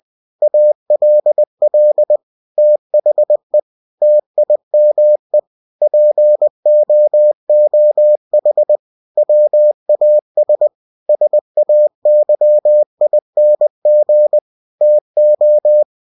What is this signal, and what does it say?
Morse code, heard as a keyed tone, about 20 wpm, 600 Hz. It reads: ALL THE TIME POOH WAS SAYING TO
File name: cw-books / winnie-the-pooh / winnie-20wpm-page-0036